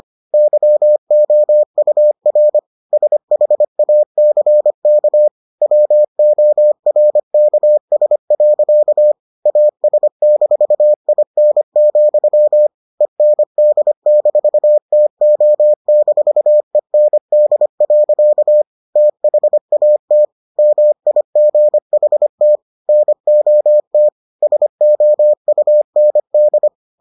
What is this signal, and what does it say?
Morse code, heard as a keyed tone, about 25 wpm, 600 Hz. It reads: YOUR SHACK WORKS. AS-IN, END-TO-END. THAT MIGHT NOT SOUND